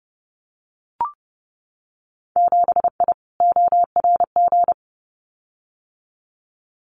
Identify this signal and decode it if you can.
Morse code, heard as a keyed tone, about 30 wpm, 700 Hz. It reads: 7I ORG
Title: Winter Field Day Exchanges Encore 30wpm